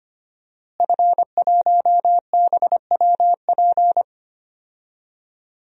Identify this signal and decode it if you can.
Morse code, heard as a keyed tone, about 25 wpm, 700 Hz. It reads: F1BWP
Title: DXCC-100 Call Signs - 5 Characters - Repeated 2x - 25wpm